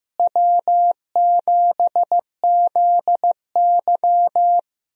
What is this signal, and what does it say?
Morse code, heard as a keyed tone, about 15 wpm, 700 Hz. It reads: W7ZY
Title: USA Call Signs - 4 Characters - Repeated 3x - 15wpm